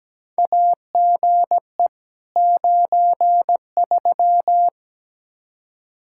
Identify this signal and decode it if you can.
Morse code, heard as a keyed tone, about 17 wpm, 700 Hz. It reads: AGE 93
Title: Practice QSO - Exchange 3 - More Info - 17wpm